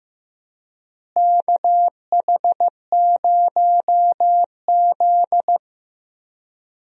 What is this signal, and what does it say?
Morse code, heard as a keyed tone, about 15 wpm, 700 Hz. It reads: KH0Z
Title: USA Call Signs - 4 Characters - Repeated 3x - 15wpm